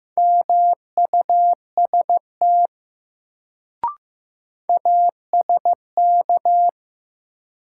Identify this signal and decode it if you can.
Morse code, heard as a keyed tone, about 16 wpm, 700 Hz. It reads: MUST ASK
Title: Sentences from Top 200 Words - New Word Drill - 15wpm